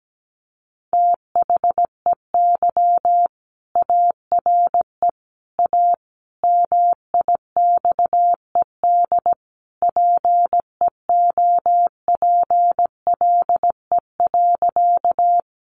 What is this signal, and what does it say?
Morse code, heard as a keyed tone, about 17 wpm, 700 Hz. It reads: THEY ARE A MIXED PEOPLE.
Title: Sentences from Top 1500 Words 17wpm